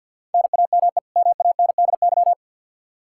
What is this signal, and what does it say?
Morse code, heard as a keyed tone, about 50 wpm, 700 Hz. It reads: NAME MANDY
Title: Practice QSO - Exchange 1 - Beginning - 50wpm